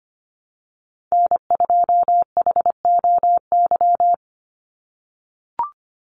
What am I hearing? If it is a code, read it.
Morse code, heard as a keyed tone, about 25 wpm, 700 Hz. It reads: N2HOY
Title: USA Call Signs 25wpm